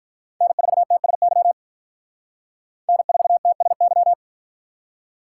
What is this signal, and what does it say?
Morse code, heard as a keyed tone, about 53 wpm, 700 Hz. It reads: N4TSY N4TSY